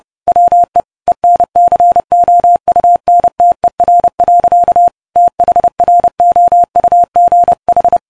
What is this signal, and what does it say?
Morse code, heard as a keyed tone, about 28 wpm, 700 Hz. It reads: WE ENCOUNTER. THROUGH